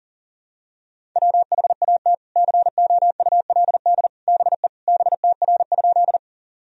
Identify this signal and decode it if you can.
Morse code, heard as a keyed tone, about 40 wpm, 700 Hz. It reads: WHAT COULD BE BTR?
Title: Sentences from Top 400 Words and Abbreviations 40wpm